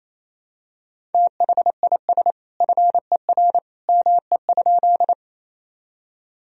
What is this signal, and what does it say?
Morse code, heard as a keyed tone, about 27 wpm, 700 Hz. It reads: THIS FER ME?
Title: Sentences from Top 200 Words and Abbreviations 28wpm